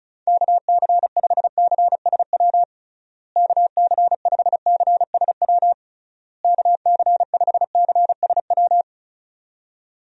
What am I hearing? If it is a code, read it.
Morse code, heard as a keyed tone, about 34 wpm, 700 Hz. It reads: KC5CSW KC5CSW KC5CSW
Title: USA Call Signs - 6 Characters - Repeated 3x - 35wpm